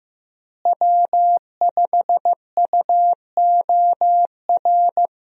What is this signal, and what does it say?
Morse code, heard as a keyed tone, about 15 wpm, 700 Hz. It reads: W5UOR
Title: USA Call Signs - 5 Characters - Repeated 3x - 15wpm